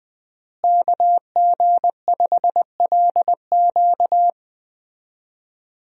Morse code, heard as a keyed tone, about 20 wpm, 700 Hz. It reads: KG5LQ
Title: USA Call Signs - 5 Characters - Repeated 3x - 20wpm